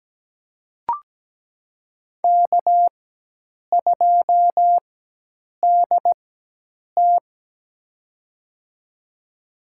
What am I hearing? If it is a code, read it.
Morse code, heard as a keyed tone, about 17 wpm, spaced as for 8 wpm, 700 Hz. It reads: K2DT
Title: USA Call Signs - 4 Characters - 4x Character Spacing - 17wpm